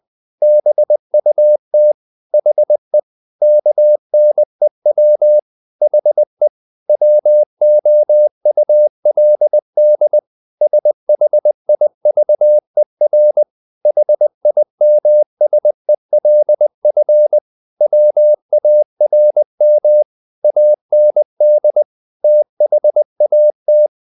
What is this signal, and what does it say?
Morse code, heard as a keyed tone, about 20 wpm, 600 Hz. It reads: BUT HE KNEW HE WOULD SHIVER HIMSELF WARM AND THAT
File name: the old man and the sea / the old man and the sea part 1 20wpm